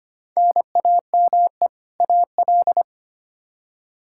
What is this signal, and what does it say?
Morse code, heard as a keyed tone, about 25 wpm, 700 Hz. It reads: NAME AL
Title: K1USN SST QSO Practice 25wpm